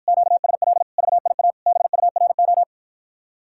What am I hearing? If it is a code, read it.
Morse code, heard as a keyed tone, about 54 wpm, 700 Hz. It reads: QSL VIA BURO